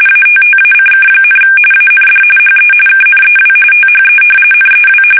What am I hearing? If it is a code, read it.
RTTY, about 45.45 baud. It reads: ATTACK, AND HE KNEW THE NEW
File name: fsk850 / battleshipnj